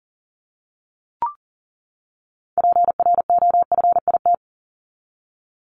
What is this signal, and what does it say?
Morse code, heard as a keyed tone, about 40 wpm, 700 Hz. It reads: PROFIT
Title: Top 751-1000 Words 40wpm